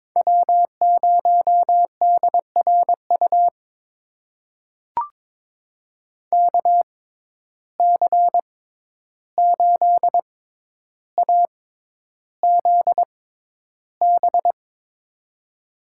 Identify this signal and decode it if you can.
Morse code, heard as a keyed tone, about 21 wpm, spaced as for 11 wpm, 700 Hz. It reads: W0DRU KC8AZB
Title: USA Call Signs - 6x Character Spacing - 22wpm